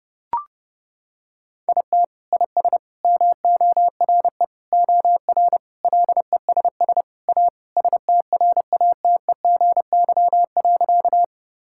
Morse code, heard as a keyed tone, about 32 wpm, 700 Hz. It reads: IT IS MORE OR LESS A STRATEGY.
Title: Sentences from Top 1000 Words - Review Part 5 - 30wpm